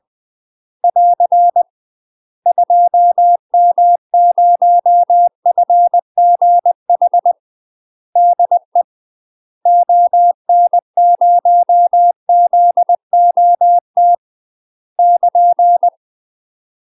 Morse code, heard as a keyed tone, about 20 wpm, 700 Hz. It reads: <AR> 2M0FGH DE ON0ZOT <KN>
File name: UK MASTER QSOs TRAINING / 20WPM / UK MASTER QSOs 0040